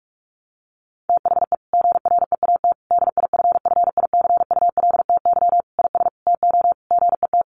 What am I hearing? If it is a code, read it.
Morse code, heard as a keyed tone, about 44 wpm, 700 Hz. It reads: THE GREAT DIFFICULTY IS TO GET
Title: Sentences from Top 1500 Words 45wpm